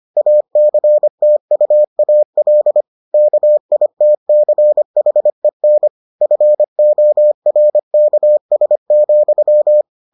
Morse code, heard as a keyed tone, about 25 wpm, 600 Hz. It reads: ACTUAL KITCHEN FORKS,